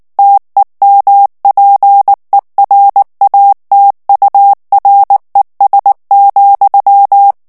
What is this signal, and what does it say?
Morse code, heard as a keyed tone, about 19 wpm, 800 Hz. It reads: TEMPERATURES,
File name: ANGLAIS texte clair 2 vitesse 18 mots minute